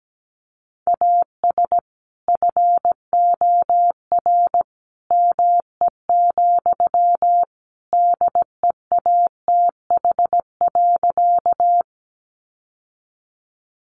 Morse code, heard as a keyed tone, about 17 wpm, 700 Hz. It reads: AS FOR ME, DEATH.
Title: Sentences from Top 1100 Words 17wpm